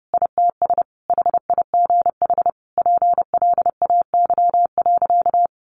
Morse code, heard as a keyed tone, about 30 wpm, 700 Hz. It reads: ITS HIGH PLAY.
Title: Sentences from Top 1500 Words 30wpm